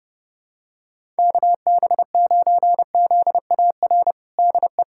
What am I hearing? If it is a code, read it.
Morse code, heard as a keyed tone, about 30 wpm, 700 Hz. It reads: KB9ZAR DE
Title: USA Callsign DE Callsign - 6 Characters - 30wpm